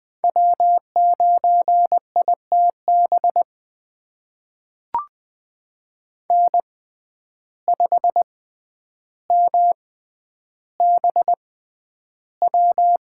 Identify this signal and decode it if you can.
Morse code, heard as a keyed tone, about 20 wpm, spaced as for 10 wpm, 700 Hz. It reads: W9ITB N5MBW
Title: USA Call Signs - 5 Characters - 6x Character Spacing - 20wpm